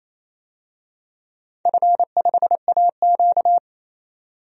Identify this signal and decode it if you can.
Morse code, heard as a keyed tone, about 28 wpm, 700 Hz. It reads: F5AQ
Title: DXCC-100 Call Signs - 4 Characters - 28wpm